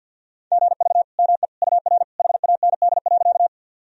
Morse code, heard as a keyed tone, about 52 wpm, 700 Hz. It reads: GV ME UR HAND.
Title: Sentences from Top 200 Words and Abbreviations 50wpm